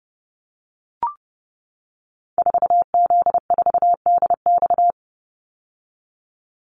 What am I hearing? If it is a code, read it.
Morse code, heard as a keyed tone, about 30 wpm, 700 Hz. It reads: 4Z4DX
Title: Call Signs - 30wpm